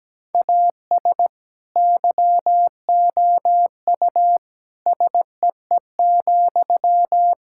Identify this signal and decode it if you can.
Morse code, heard as a keyed tone, about 17 wpm, 700 Hz. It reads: AS YOU SEE,
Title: Sentences from Top 1000 Words - Review Part 6 - 17wpm